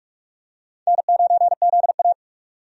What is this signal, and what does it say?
Morse code, heard as a keyed tone, about 44 wpm, 700 Hz. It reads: N9ZA
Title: USA Call Signs - 4 Characters - 45wpm